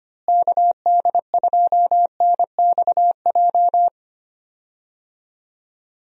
Morse code, heard as a keyed tone, about 25 wpm, 700 Hz. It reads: KD2NXJ
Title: USA Call Signs - 6 Characters - Repeated 2x - 25wpm